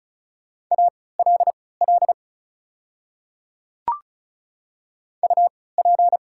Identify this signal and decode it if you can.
Morse code, heard as a keyed tone, about 36 wpm, spaced as for 20 wpm, 700 Hz. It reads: ALL UP
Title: 3x Character Spacing - Top 100 Words - 35wpm